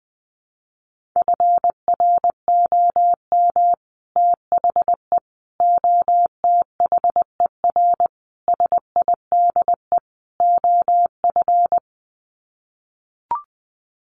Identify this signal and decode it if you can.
Morse code, heard as a keyed tone, about 20 wpm, 700 Hz. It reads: FROM THE OTHER SIDE OF
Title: Sets of 5 Words Encore 20wpm